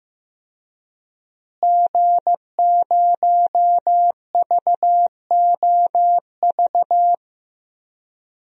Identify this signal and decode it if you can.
Morse code, heard as a keyed tone, about 15 wpm, 700 Hz. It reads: G0VOV
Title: DXCC-100 Call Signs - 5 Characters - Repeated 2x - 15wpm